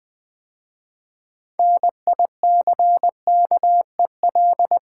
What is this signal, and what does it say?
Morse code, heard as a keyed tone, about 20 wpm, 700 Hz. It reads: NICKEL
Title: 6 Letter Words - First 3 Letters Given - 20wpm